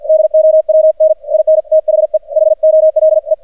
Morse code, heard as a keyed tone, about 46 wpm, 630 Hz. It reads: DOWN IN THE HOLE
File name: archived / CRSnet